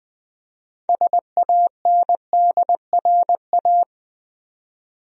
Morse code, heard as a keyed tone, about 20 wpm, 700 Hz. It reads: SANDRA
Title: USA Common Names - 20wpm